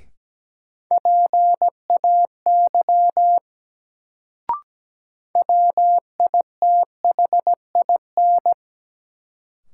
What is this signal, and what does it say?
Morse code, heard as a keyed tone, about 17 wpm, 700 Hz. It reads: PAY WITHIN
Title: Sentences from Top 700 Words - New Word Drill - 17wpm